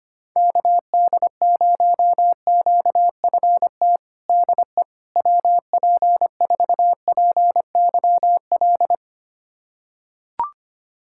Morse code, heard as a keyed tone, about 25 wpm, 700 Hz. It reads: KD0QFT DE WP4PYL
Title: USA Callsign DE Callsign - 6 Characters - 25wpm